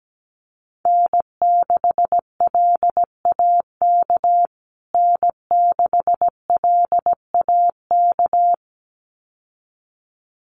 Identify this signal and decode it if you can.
Morse code, heard as a keyed tone, about 17 wpm, 700 Hz. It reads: N6LAK N6LAK
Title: USA Call Signs Repeated 2x 17wpm